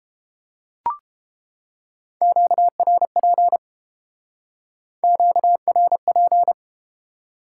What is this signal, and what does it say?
Morse code, heard as a keyed tone, about 30 wpm, 700 Hz. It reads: QRP QRP